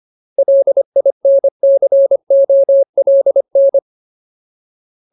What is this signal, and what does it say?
Morse code, heard as a keyed tone, about 25 wpm, 550 Hz. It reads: LINCOLN
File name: Ultimate Word List / 25WPM Ultimate